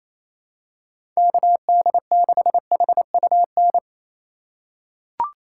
Morse code, heard as a keyed tone, about 28 wpm, 700 Hz. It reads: KD6HUN
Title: USA Call Signs - 6 Characters - 4x Character Spacing - 28wpm